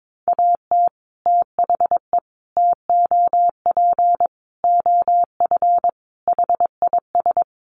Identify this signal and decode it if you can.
Morse code, heard as a keyed tone, about 22 wpm, 700 Hz. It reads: AT THE TOP OF HIS
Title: Sets of 5 Words Encore 22wpm